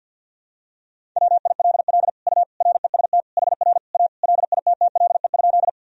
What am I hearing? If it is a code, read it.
Morse code, heard as a keyed tone, about 52 wpm, 700 Hz. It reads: WILL U REST HR A LITTLE?
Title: Sentences from Top 500 Words and Abbreviations 50wpm